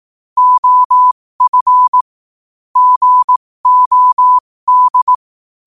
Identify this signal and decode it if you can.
Morse code, heard as a keyed tone, about 18 wpm, spaced as for 15 wpm, 1000 Hz. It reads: OF GOD
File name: qotdfiles / 15wpm